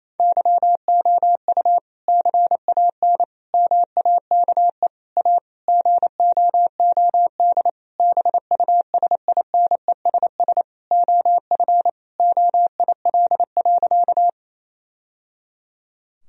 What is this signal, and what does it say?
Morse code, heard as a keyed tone, about 28 wpm, 700 Hz. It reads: YOU CAN MAKE A GOOD BUSINESS OF OIL.
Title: Sentences from Top 1000 Words - Review Part 4 - 28wpm